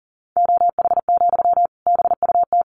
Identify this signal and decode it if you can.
Morse code, heard as a keyed tone, about 40 wpm, 700 Hz. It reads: OH, BUT
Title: Sentences from Top 1300 Words 40wpm